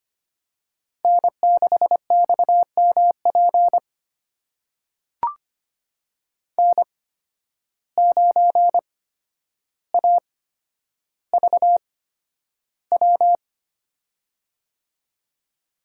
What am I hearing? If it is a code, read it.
Morse code, heard as a keyed tone, about 25 wpm, spaced as for 11 wpm, 700 Hz. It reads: N6XMP N9AVW
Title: USA Call Signs - 5 Characters - 8x Character Spacing - 25wpm